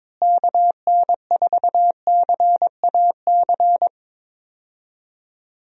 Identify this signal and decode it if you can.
Morse code, heard as a keyed tone, about 22 wpm, 700 Hz. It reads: KN4CAC
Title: USA Call Signs - 6 Characters - Repeated 2x - 22wpm